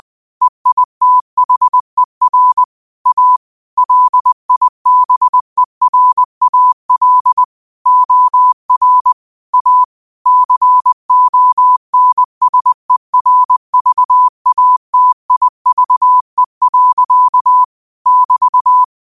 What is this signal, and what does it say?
Morse code, heard as a keyed tone, about 20 wpm, 1000 Hz. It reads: EITHER A LIBERAL OR A CONSERVATIVE. <BT>
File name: qotdfiles / 20wpm